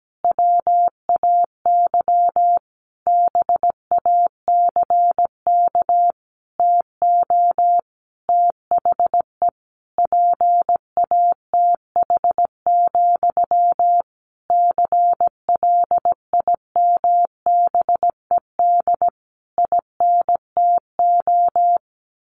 Morse code, heard as a keyed tone, about 17 wpm, 700 Hz. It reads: WAY BACK TO THE PATH, CLIMBED INTO
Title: A Sound Of Thunder 17wpm